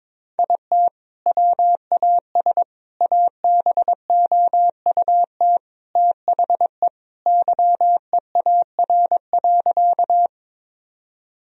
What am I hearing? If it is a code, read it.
Morse code, heard as a keyed tone, about 22 wpm, 700 Hz. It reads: IT WAS ABOUT THE YEAR.
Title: Sentences from Top 1000 Words - Review Part 1 - 22wpm